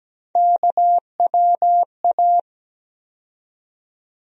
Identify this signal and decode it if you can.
Morse code, heard as a keyed tone, about 17 wpm, 700 Hz. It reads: KWA
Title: USA Call Sign Postfixes - 3 Letters 17wpm